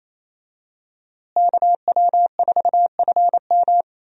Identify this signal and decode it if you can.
Morse code, heard as a keyed tone, about 28 wpm, 700 Hz. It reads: KW4FM
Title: USA Call Signs - 5 Characters - Repeated 3x - 28wpm